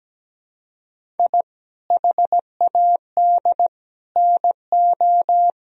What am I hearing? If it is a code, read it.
Morse code, heard as a keyed tone, about 17 wpm, 700 Hz. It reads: I HAD NO
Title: Sentences from Top 1000 Words - Review Part 1 - 17wpm